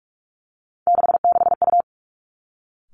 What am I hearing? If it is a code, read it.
Morse code, heard as a keyed tone, about 46 wpm, 700 Hz. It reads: 66U
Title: Three Challenging Letter-Number Combinations 45wpm